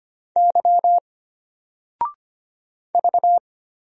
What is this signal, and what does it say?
Morse code, heard as a keyed tone, about 25 wpm, 700 Hz. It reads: Y V
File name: Single-Letter-Number-25wpm